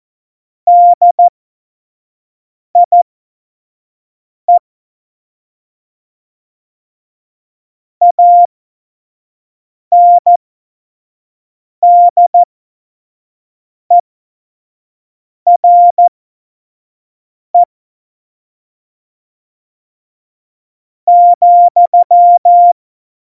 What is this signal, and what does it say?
Morse code, heard as a keyed tone, about 14 wpm, spaced as for 5 wpm, 700 Hz. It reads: DIE ANDERE ,
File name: Morsekurs / Witze / Witz 24.Juli 2018, 25 F70